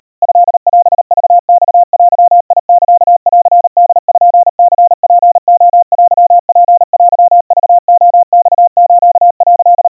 Morse code (Japanese Wabun code), heard as a keyed tone, about 39 wpm, 700 Hz. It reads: トオクマデサンポニツレテツテクレマス」
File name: kawa / Morse sample / kamyu 140 cpm 700 Hz Crispy